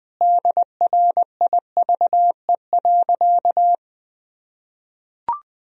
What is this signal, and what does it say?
Morse code, heard as a keyed tone, about 20 wpm, 700 Hz. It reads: DRIVE.
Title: Sentences from Top 1000 Words - Review Part 2 - 20wpm